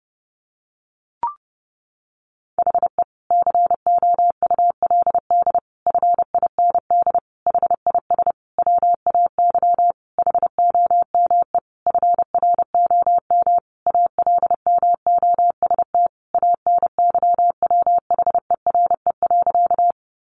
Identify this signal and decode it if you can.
Morse code, heard as a keyed tone, about 30 wpm, 700 Hz. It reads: HE COULD FIND HIS WAY HOME FROM ALMOST ANYWHERE.
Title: Sentences from Top 1400 Words 30wpm